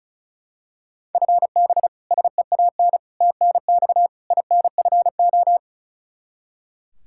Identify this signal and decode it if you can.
Morse code, heard as a keyed tone, about 36 wpm, 700 Hz. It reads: FB SEAN TNX INFO
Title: Practice QSO - Exchange 3 - More Info - 35wpm